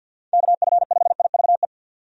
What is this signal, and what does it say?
Morse code, heard as a keyed tone, about 45 wpm, 700 Hz. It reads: KF5IVE
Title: USA Call Signs - 6 Characters - 50wpm